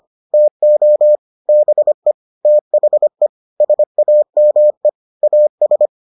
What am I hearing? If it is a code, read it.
Morse code, heard as a keyed tone, about 25 wpm, 600 Hz. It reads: TO BE THE SAME AS